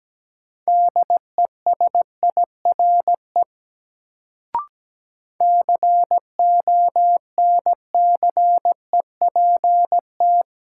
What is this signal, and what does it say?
Morse code, heard as a keyed tone, about 17 wpm, 700 Hz. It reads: DESIRE CONCEPT
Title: Sentences from Top 1100 Words - New Word Drill - 17wpm